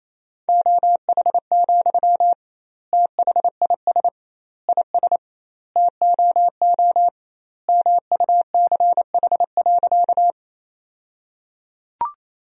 Morse code, heard as a keyed tone, about 28 wpm, 700 Hz. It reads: OH, THIS IS TOO MUCH.
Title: 2x Word-Spacing - Sentences from Top 500 Words - 28wpm